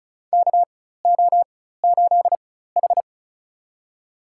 Morse code, heard as a keyed tone, about 36 wpm, spaced as for 16 wpm, 700 Hz. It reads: KO8H
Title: USA Call Signs - 4x Character Spacing - 35wpm